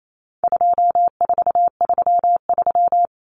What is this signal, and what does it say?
Morse code, heard as a keyed tone, about 28 wpm, 700 Hz. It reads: 2433
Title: Common Cut Numbers - 4-Digit 28wpm